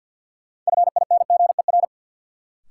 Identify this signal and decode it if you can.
Morse code, heard as a keyed tone, about 50 wpm, 700 Hz. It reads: FINGER
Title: Sentences from Top 1600 Words - New Word Drill Alt - 50wpm